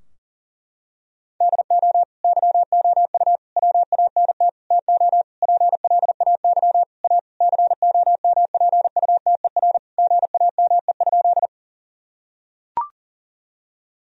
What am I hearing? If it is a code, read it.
Morse code, heard as a keyed tone, about 37 wpm, 700 Hz. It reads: DO YOU WANT TO PLAY A COMPUTER GAME?
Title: Sentences from Top 1000 Words - Review Part 3 - 40wpm